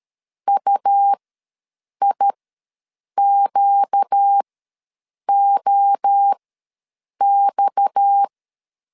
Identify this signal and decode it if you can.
Morse code, heard as a keyed tone, about 13 wpm, spaced as for 7 wpm, 800 Hz. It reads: UIQOX
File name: LEZIONI / 6 PKX0 / 6 PKX ws 12wpm 3106